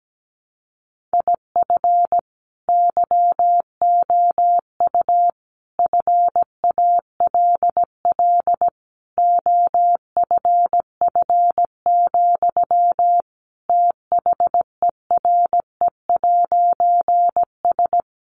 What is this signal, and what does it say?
Morse code, heard as a keyed tone, about 17 wpm, 700 Hz. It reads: IF YOU FALL OFF, THERE'S
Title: A Sound Of Thunder 17wpm